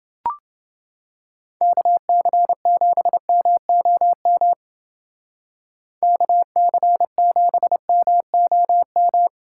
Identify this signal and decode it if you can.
Morse code, heard as a keyed tone, about 29 wpm, 700 Hz. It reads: KC7MOM KC7MOM